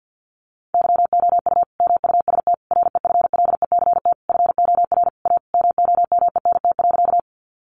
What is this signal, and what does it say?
Morse code, heard as a keyed tone, about 48 wpm, 700 Hz. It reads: YOU MUST REFLECT FOR A MOMENT.
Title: Sentences from Top 1500 Words 50wpm